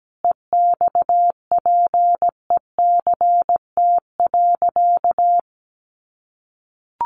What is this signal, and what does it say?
Morse code, heard as a keyed tone, about 17 wpm, 700 Hz. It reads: EXPECT.
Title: Sentences from Top 1100 Words 17wpm